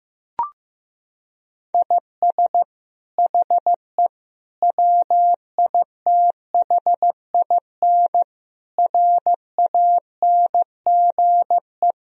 Morse code, heard as a keyed tone, about 16 wpm, 700 Hz. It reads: IS HE WITHIN RANGE
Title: Sentences from Top 1000 Words - Review Part 4 - 15wpm